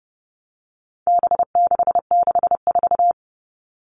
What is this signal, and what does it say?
Morse code, heard as a keyed tone, about 30 wpm, 700 Hz. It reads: B664
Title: All Cut Numbers - 4-Digit 30wpm